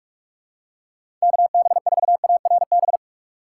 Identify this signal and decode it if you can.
Morse code, heard as a keyed tone, about 46 wpm, 700 Hz. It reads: KB4ARB
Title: USA Call Signs - 6 Characters - 3x Character Spacing - 45wpm